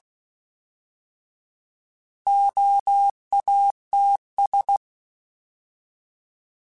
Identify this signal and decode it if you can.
Morse code, heard as a keyed tone, about 16 wpm, 800 Hz. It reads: OATS
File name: Alphabetical / Alphabetical mp3s / Os @15